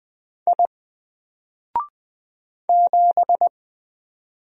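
Morse code, heard as a keyed tone, about 20 wpm, 700 Hz. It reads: I 7
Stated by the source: Single letter and number Morse drill at 20 wpm, from N4LCD.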